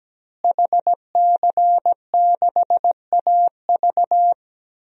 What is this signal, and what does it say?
Morse code, heard as a keyed tone, about 17 wpm, 700 Hz. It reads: HC6AV
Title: DXCC-100 Call Signs - 5 Characters - Repeated 2x - 17wpm